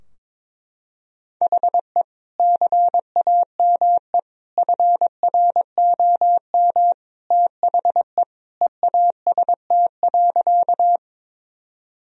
Morse code, heard as a keyed tone, about 21 wpm, 700 Hz. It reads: HE CAME FROM THE EAST.
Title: Sentences from Top 1000 Words - Review Part 3 - 22wpm